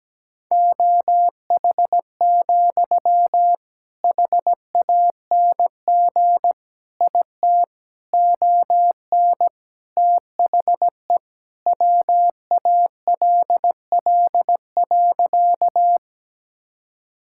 Morse code, heard as a keyed tone, about 17 wpm, 700 Hz. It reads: OH, HANG IT ON THE WALL.
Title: Sentences from Top 1600 Words 17wpm